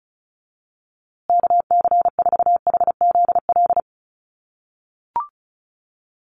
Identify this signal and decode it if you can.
Morse code, heard as a keyed tone, about 34 wpm, 700 Hz. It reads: KC4HZL
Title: USA Call Signs Repeated 3x 35wpm